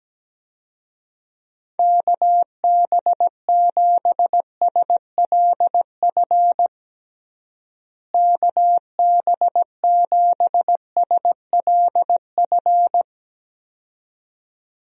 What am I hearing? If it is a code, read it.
Morse code, heard as a keyed tone, about 17 wpm, 700 Hz. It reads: KB7SLF KB7SLF
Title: USA Call Signs - 6 Characters - Repeated 2x - 17wpm